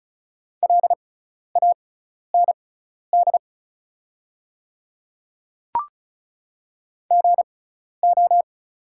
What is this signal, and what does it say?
Morse code, heard as a keyed tone, about 36 wpm, spaced as for 12 wpm, 690 Hz. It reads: LAND GO
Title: 6x Character Spacing - Top 300 Words - 35wpm